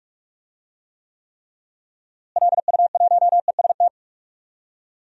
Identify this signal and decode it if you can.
Morse code, heard as a keyed tone, about 46 wpm, 700 Hz. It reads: LU1EST
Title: DXCC-100 Call Signs - 6 Characters - Repeated 3x - 45wpm